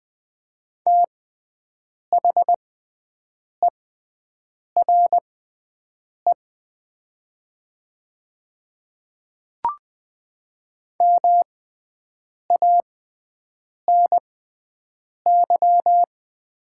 Morse code, heard as a keyed tone, about 20 wpm, spaced as for 7 wpm, 700 Hz. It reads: THERE MANY
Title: 6x Character Spacing - Top 100 Words - 20wpm